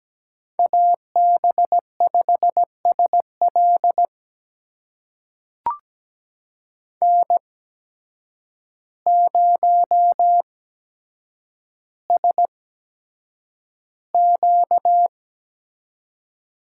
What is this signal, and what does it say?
Morse code, heard as a keyed tone, about 17 wpm, spaced as for 8 wpm, 700 Hz. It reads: AB5SL N0SQ
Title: USA Call Signs - 5 Characters - 8x Character Spacing - 17wpm